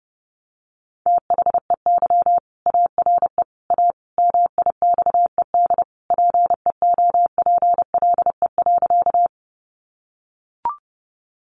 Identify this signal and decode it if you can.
Morse code, heard as a keyed tone, about 30 wpm, 700 Hz. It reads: THEY ARE A MIXED PEOPLE.
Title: Sentences from Top 1500 Words 30wpm